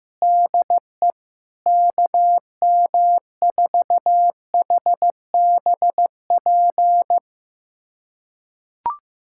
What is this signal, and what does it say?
Morse code, heard as a keyed tone, about 15 wpm, 700 Hz. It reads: DE KM4HBP
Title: USA Callsign DE Callsign - 6 Characters - 15wpm